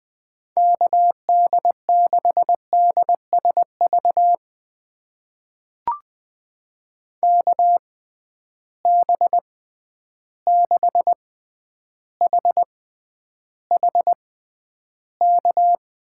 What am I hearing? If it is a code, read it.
Morse code, heard as a keyed tone, about 20 wpm, spaced as for 10 wpm, 700 Hz. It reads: KD6DSV KB6HHK
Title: USA Call Signs - 6 Characters - 6x Character Spacing - 20wpm